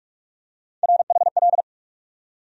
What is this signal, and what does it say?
Morse code, heard as a keyed tone, about 46 wpm, 700 Hz. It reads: RHL
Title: USA Call Sign Postfixes - 3 Letters 45wpm